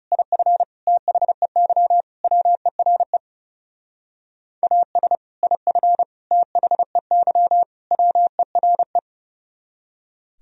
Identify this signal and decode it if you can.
Morse code, heard as a keyed tone, about 36 wpm, 700 Hz. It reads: IF THEY WERE AS IF THEY WERE
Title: Speed-Racing - Sets of 4 Words 30wpm